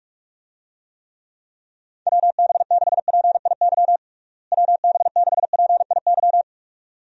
Morse code, heard as a keyed tone, about 41 wpm, 700 Hz. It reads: WB6PIY WB6PIY
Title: USA Call Signs - 6 Characters - Repeated 2x - 45wpm